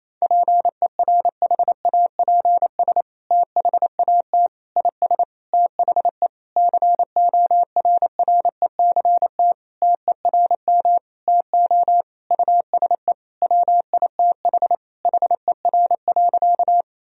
Morse code, heard as a keyed tone, about 28 wpm, 700 Hz. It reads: PERHAPS THAT IS THE CORRECT TERM TO USE WITH HER.
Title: Sentences from Top 1000 Words - Review Part 4 - 28wpm